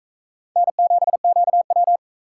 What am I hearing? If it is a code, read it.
Morse code, heard as a keyed tone, about 42 wpm, 700 Hz. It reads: N7QW